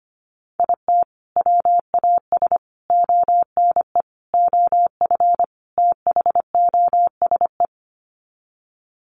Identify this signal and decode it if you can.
Morse code, heard as a keyed tone, about 25 wpm, 700 Hz. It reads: IT WAS ONE OF THOSE
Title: Sets of 5 Words Encore 25wpm